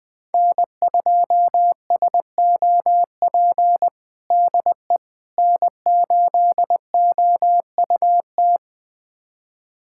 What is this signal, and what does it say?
Morse code, heard as a keyed tone, about 20 wpm, 700 Hz. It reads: N2SOP DE N8OUT
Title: USA Callsign DE Callsign - 5 Characters - 20wpm